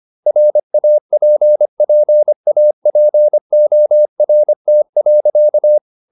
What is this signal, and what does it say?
Morse code, heard as a keyed tone, about 25 wpm, 600 Hz. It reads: RAPPAPORT.